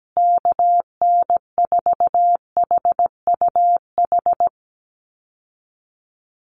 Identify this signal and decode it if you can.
Morse code, heard as a keyed tone, about 17 wpm, 700 Hz. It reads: KN4HUH
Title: USA Call Signs Repeated 3x 17wpm